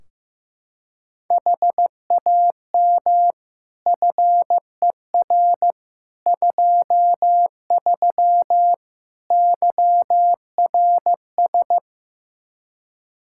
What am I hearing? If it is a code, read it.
Morse code, heard as a keyed tone, about 16 wpm, 700 Hz. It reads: HAM FER 23 YRS
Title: Practice QSO - Exchange 3 - More Info - 15wpm